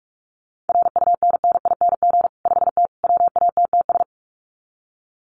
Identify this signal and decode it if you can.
Morse code, heard as a keyed tone, about 44 wpm, 700 Hz. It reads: RUNNING 5T WATTS
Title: Common QSO Phrases 45wpm